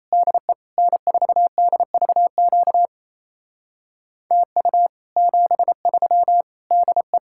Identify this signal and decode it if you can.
Morse code, heard as a keyed tone, about 34 wpm, 700 Hz. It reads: DE N4DVQ TU 73 DE
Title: Speed-Racing Common QSO Phrases - 28wpm